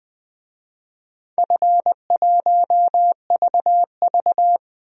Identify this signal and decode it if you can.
Morse code, heard as a keyed tone, about 21 wpm, 700 Hz. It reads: F1VV
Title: DXCC-100 Call Signs - 4 Characters - Repeated 3x - 20wpm